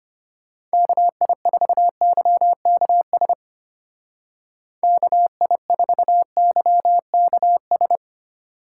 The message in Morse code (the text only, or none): KI4YKS KI4YKS